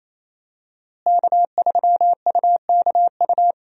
Morse code, heard as a keyed tone, about 28 wpm, 700 Hz. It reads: K3UKU
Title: USA Call Signs - 5 Characters - Repeated 3x - 28wpm